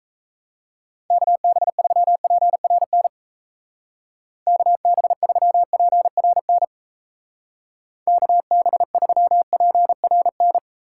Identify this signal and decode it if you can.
Morse code, heard as a keyed tone, about 34 wpm, 700 Hz. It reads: KB3PRN KB3PRN KB3PRN
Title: Speed Racing - USA 2x3 Call Signs 28wpm